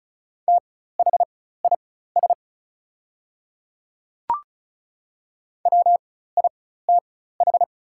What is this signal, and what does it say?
Morse code, heard as a keyed tone, about 34 wpm, spaced as for 16 wpm, 700 Hz. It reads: THIS WITH
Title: 4x Character Spacing - Top 100 Words - 35wpm